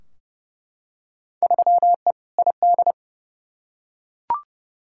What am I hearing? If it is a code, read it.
Morse code, heard as a keyed tone, about 30 wpm, 700 Hz. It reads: 3E ID
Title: Field Day Exchanges Encore - 30wpm